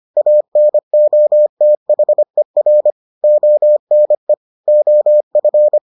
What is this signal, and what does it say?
Morse code, heard as a keyed tone, about 25 wpm, 600 Hz. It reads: ANOTHER ONE OF